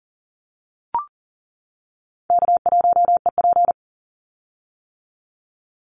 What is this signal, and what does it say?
Morse code, heard as a keyed tone, about 40 wpm, 700 Hz. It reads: K1EP
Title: Call Signs - 40wpm